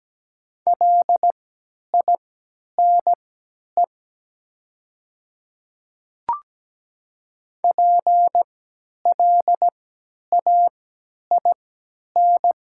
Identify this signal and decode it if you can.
Morse code, heard as a keyed tone, about 17 wpm, spaced as for 10 wpm, 700 Hz. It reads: LINE PLAIN
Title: 3x Character Spacing - Top 500 Words - 17wpm